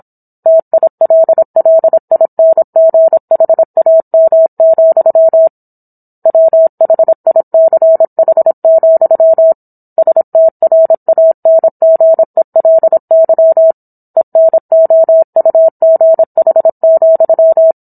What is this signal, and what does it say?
Morse code, heard as a keyed tone, about 27 wpm, 650 Hz. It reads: TILLINGHAM, WHICH, STRANGELY ENOUGH,